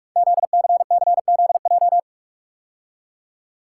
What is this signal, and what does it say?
Morse code, heard as a keyed tone, about 44 wpm, 700 Hz. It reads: ZCCZJ
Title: Code Groups - 5 Letters - 45wpm